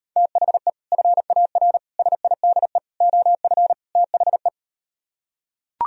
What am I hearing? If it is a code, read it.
Morse code, heard as a keyed tone, about 39 wpm, 700 Hz. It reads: THE FAR SIDE OF THE